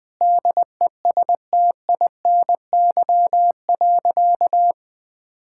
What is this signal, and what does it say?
Morse code, heard as a keyed tone, about 20 wpm, 700 Hz. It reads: DESTINY.
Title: Sentences with 6-Word Phrases - 20wpm